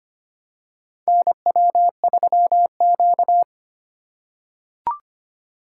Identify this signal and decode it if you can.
Morse code, heard as a keyed tone, about 25 wpm, 700 Hz. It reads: NW3Q
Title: USA Call Signs - 4 Characters - 3x Character Spacing - 25wpm